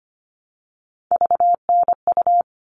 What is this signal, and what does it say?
Morse code, heard as a keyed tone, about 25 wpm, 700 Hz. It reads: VNU
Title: All Cut Numbers - 3-Digit 25wpm